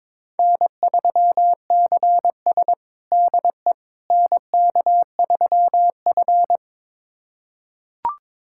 Morse code, heard as a keyed tone, about 22 wpm, 700 Hz. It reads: N3CS DE NK3F
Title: USA Callsign DE Callsign - 4 Characters - 22wpm